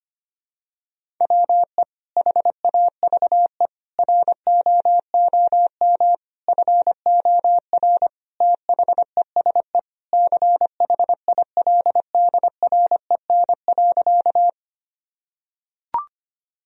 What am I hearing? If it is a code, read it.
Morse code, heard as a keyed tone, about 25 wpm, 700 Hz. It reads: WE HAVE ROOM FOR THESE CHILDREN.
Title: Sentences from Top 1000 Words - Review Part 1 - 25wpm